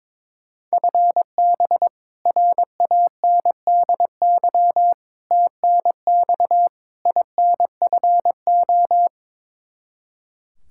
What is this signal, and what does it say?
Morse code, heard as a keyed tone, about 22 wpm, 700 Hz. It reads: FB RANDY TNX INFO
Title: K1USN SST QSO Practice 22wpm